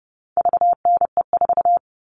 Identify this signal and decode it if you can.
Morse code, heard as a keyed tone, about 30 wpm, 700 Hz. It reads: VNE4
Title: All Cut Numbers - 4-Digit 30wpm